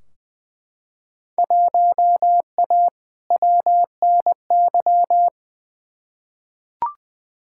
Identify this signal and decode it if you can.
Morse code, heard as a keyed tone, about 20 wpm, 700 Hz. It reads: 1A WNY
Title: Field Day Exchanges - 20wpm